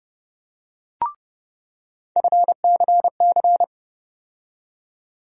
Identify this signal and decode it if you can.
Morse code, heard as a keyed tone, about 30 wpm, 700 Hz. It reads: FCC
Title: USA Call Sign Postfixes - 3 Letters 30wpm